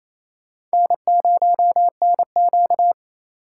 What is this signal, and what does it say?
Morse code, heard as a keyed tone, about 35 wpm, 700 Hz. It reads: N0NQ